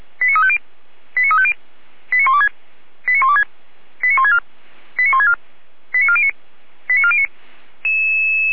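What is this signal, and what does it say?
Five-tone selective call (ZVEI), heard as quick runs of five tones, tones of about 80 ms. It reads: 89350 89350 89126 89126 89164 89164 89409 89409
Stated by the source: ZVEI five-tone selective call received with an SDR.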